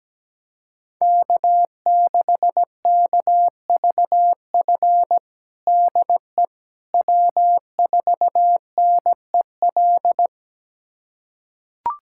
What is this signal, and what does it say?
Morse code, heard as a keyed tone, about 17 wpm, 700 Hz. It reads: K6KVF DE W4NEL
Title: USA Callsign DE Callsign - 5 Characters - 17wpm